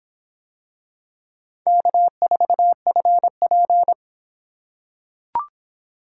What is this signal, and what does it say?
Morse code, heard as a keyed tone, about 25 wpm, 700 Hz. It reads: K4FP